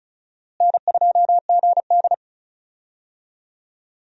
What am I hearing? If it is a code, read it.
Morse code, heard as a keyed tone, about 34 wpm, 700 Hz. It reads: N2GD
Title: USA Call Signs - 4 Characters - Repeated 2x - 35wpm